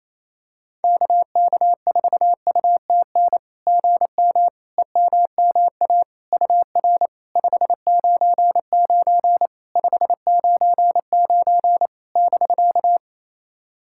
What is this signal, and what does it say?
Morse code, heard as a keyed tone, about 28 wpm, 700 Hz. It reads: KK4UTN GM EMMA UR 599 599 <BK>
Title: SOTA QSO Practice 28wpm